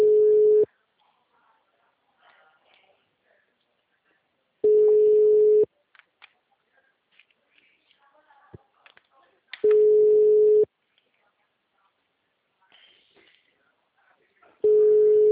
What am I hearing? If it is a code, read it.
Ringback tone (Europe, 425 Hz), with speech.
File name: out- -2013- - - .8